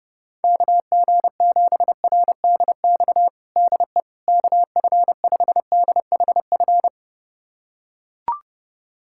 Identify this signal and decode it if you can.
Morse code, heard as a keyed tone, about 32 wpm, 700 Hz. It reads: KG7RDX DE KF5DHF
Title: USA Callsign DE Callsign - 6 Characters - 30wpm